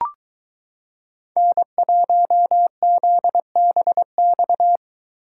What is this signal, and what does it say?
Morse code, heard as a keyed tone, about 22 wpm, 700 Hz. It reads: N1ZBX